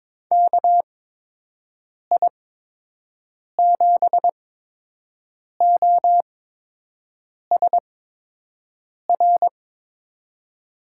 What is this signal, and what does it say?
Morse code, heard as a keyed tone, about 22 wpm, spaced as for 6 wpm, 700 Hz. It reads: KI7OSR
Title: USA Call Signs - 6 Characters - 8x Character Spacing - 22wpm